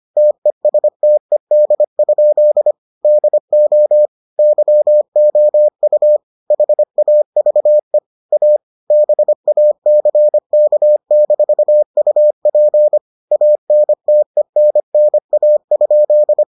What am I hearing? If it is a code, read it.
Morse code, heard as a keyed tone, about 25 wpm, 600 Hz. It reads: TESTED? DO YOU HAVE A BACK-UP ANTENNA?